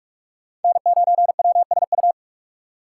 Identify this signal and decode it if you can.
Morse code, heard as a keyed tone, about 48 wpm, 700 Hz. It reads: N9WSU
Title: USA Call Signs - 5 Characters - 3x Character Spacing - 45wpm